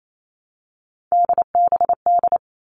Morse code, heard as a keyed tone, about 28 wpm, 700 Hz. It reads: DBD
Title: All Cut Numbers - 3-Digit 28wpm